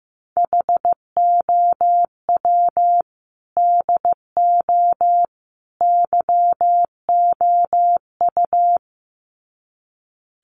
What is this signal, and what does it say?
Morse code, heard as a keyed tone, about 15 wpm, 700 Hz. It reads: HOW DO YOU
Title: Sets of 3 Words 15wpm